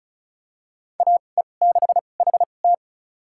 Morse code, heard as a keyed tone, about 36 wpm, spaced as for 25 wpm, 700 Hz. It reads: AE6HT
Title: USA Call Signs - 2x Character Spacing - 35wpm